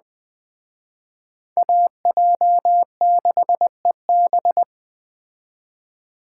Morse code, heard as a keyed tone, about 20 wpm, 700 Hz. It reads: AJ6EB
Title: Speed Racing - USA 2x2 Call Signs 17wpm